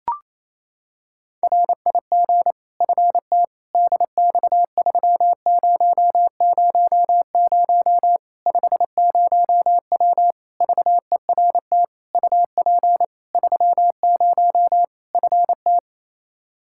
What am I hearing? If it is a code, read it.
Morse code, heard as a keyed tone, about 28 wpm, 700 Hz. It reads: RIG FT DX3000 50W VERT UP 30 FT
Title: Practice QSO - Exchange 2 - Info - 28wpm